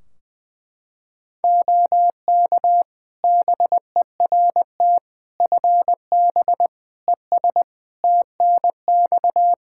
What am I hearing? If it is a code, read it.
Morse code, heard as a keyed tone, about 20 wpm, 700 Hz. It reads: OK BERT FB ES TNX
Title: Practice QSO - Exchange 2 - Info - 20wpm